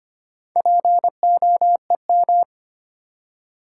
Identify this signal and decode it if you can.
Morse code, heard as a keyed tone, about 25 wpm, 700 Hz. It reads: POEM
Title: Four Letter Words - 25wpm